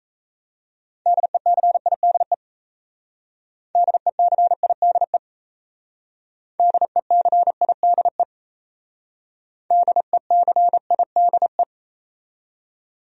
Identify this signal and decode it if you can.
Morse code, heard as a keyed tone, about 36 wpm, 700 Hz. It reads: DECIDE DECIDE DECIDE DECIDE